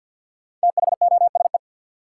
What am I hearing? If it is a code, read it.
Morse code, heard as a keyed tone, about 53 wpm, 700 Hz. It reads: THOSE
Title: Sentences from Top 300 Words - New Word Drill - 50wpm